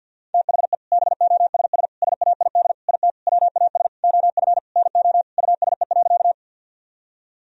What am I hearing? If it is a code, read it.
Morse code, heard as a keyed tone, about 50 wpm, 700 Hz. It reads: THE BOSS SAID IT WAS OF NO USE.
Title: Sentences from Top 1000 Words - Review Part 4 - 50wpm